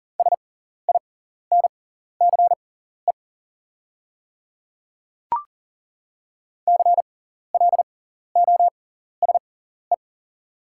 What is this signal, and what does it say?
Morse code, heard as a keyed tone, about 40 wpm, spaced as for 14 wpm, 700 Hz. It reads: SINCE CLOSE
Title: 6x Character Spacing - Top 300 Words - 40wpm